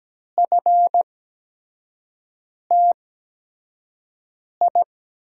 Morse code, heard as a keyed tone, about 17 wpm, spaced as for 5 wpm, 700 Hz. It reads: FTI
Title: USA Call Signs - 8x Character Spacing - 17wpm